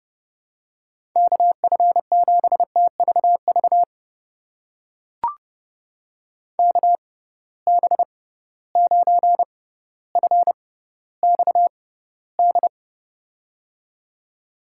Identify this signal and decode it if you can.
Morse code, heard as a keyed tone, about 32 wpm, spaced as for 16 wpm, 700 Hz. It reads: KF7TVV KB9FXD
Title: USA Call Signs - 6 Characters - 6x Character Spacing - 30wpm